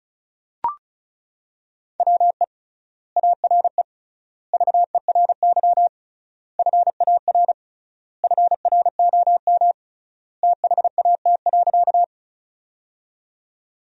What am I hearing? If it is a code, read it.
Morse code, heard as a keyed tone, about 36 wpm, 700 Hz. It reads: WE ARE VERY FAR FROM THAT.
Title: 3x Word-Spacing - Sentences from Top 300 Words - 35wpm